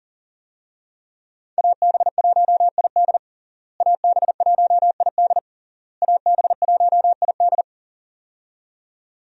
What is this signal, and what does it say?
Morse code, heard as a keyed tone, about 41 wpm, 700 Hz. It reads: AB1ID AB1ID AB1ID
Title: USA Call Signs - 5 Characters - Repeated 3x - 40wpm